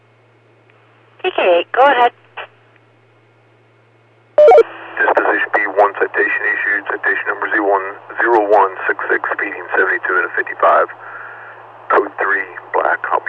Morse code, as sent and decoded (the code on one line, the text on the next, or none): none